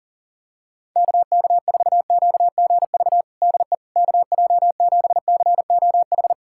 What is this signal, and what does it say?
Morse code, heard as a keyed tone, about 40 wpm, 700 Hz. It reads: KK4QGV DE KJ7COH
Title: USA Callsign DE Callsign 40wpm